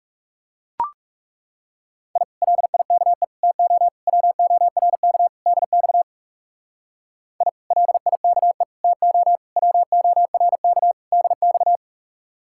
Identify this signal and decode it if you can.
Morse code, heard as a keyed tone, about 37 wpm, 700 Hz. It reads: I LIKE TO WORK DX I LIKE TO WORK DX
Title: Speed-Racing Common QSO Phrases Encore - 30wpm